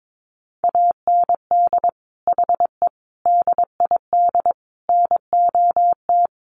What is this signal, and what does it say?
Morse code, heard as a keyed tone, about 22 wpm, 700 Hz. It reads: AND HE DID NOT
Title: Sentences from Top 1500 Words 22wpm